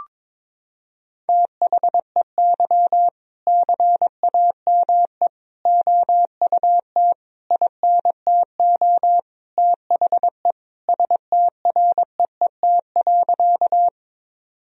Morse code, heard as a keyed tone, about 22 wpm, 700 Hz. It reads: THEY CAME OUT INTO THE STREET.
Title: Sentences from Top 1000 Words - Review Part 3 - 22wpm